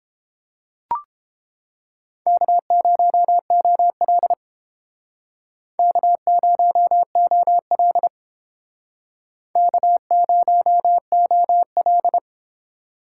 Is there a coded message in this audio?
Yes: Morse code from 2.3 to 12.2 s, heard as a keyed tone, about 31 wpm, 700 Hz. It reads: K0OL K0OL K0OL